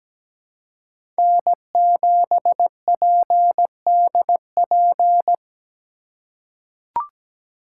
Morse code, heard as a keyed tone, about 17 wpm, 700 Hz. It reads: N7PDP
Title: USA Call Signs - 5 Characters - 3x Character Spacing - 17wpm